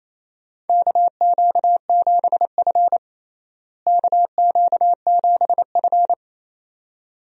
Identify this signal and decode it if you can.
Morse code, heard as a keyed tone, about 28 wpm, 700 Hz. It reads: KQ7F KQ7F
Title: USA Call Signs - 4 Characters - Repeated 2x - 28wpm